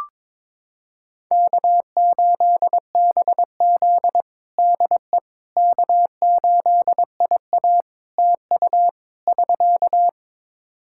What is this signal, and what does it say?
Morse code, heard as a keyed tone, about 22 wpm, 700 Hz. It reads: K8BZ DE K8IA TU <SK>
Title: K1USN SST QSO Practice 22wpm